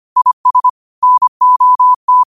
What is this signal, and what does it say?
Morse code, heard as a keyed tone, about 25 wpm, 1000 Hz. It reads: IS NOT
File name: qotdfiles / 25wpm